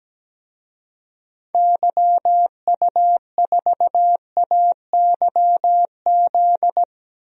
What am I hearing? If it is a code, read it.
Morse code, heard as a keyed tone, about 17 wpm, 700 Hz. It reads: YU4AYZ
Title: DXCC-100 Call Signs - 6 Characters - Repeated 3x - 17wpm